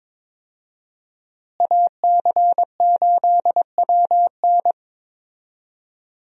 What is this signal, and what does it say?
Morse code, heard as a keyed tone, about 22 wpm, 700 Hz. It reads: AC8WN
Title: USA Call Signs - 5 Characters - Repeated 2x - 22wpm